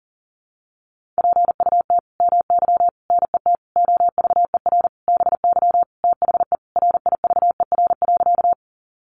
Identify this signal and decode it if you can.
Morse code, heard as a keyed tone, about 37 wpm, 700 Hz. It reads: PUT MY NET OVER BY THE RIVER.
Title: Sentences from Top 1400 Words 40wpm